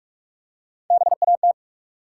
Morse code, heard as a keyed tone, about 47 wpm, 700 Hz. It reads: BAT
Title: Three Letter Words - 45wpm